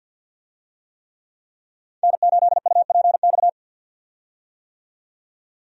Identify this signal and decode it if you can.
Morse code, heard as a keyed tone, about 52 wpm, 700 Hz. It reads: N8UPX
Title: USA Call Signs - 5 Characters - 50wpm